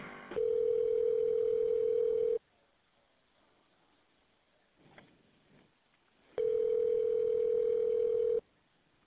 Ringback tone (North America), 440 + 480 Hz.